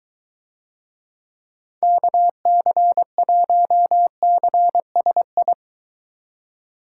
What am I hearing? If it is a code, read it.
Morse code, heard as a keyed tone, about 21 wpm, 700 Hz. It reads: KC1CSI